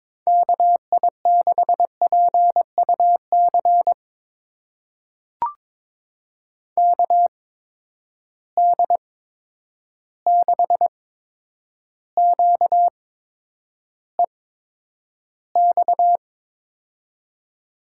Morse code, heard as a keyed tone, about 22 wpm, spaced as for 9 wpm, 700 Hz. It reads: KI6PUC KD6QEX
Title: USA Call Signs - 6 Characters - 8x Character Spacing - 22wpm